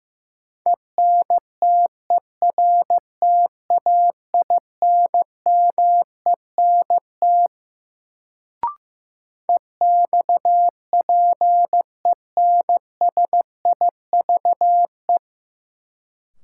Morse code, heard as a keyed tone, about 15 wpm, 700 Hz. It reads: ENTERTAINMENT EXPENSIVE
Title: Sentences from Top 1000 Words - Review Part 6 - New Word Drill - 15wpm